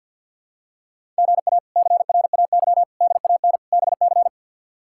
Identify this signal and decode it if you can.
Morse code, heard as a keyed tone, about 46 wpm, 700 Hz. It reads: GA CRAY DAN BC
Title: K1USN SST QSO Practice - 50wpm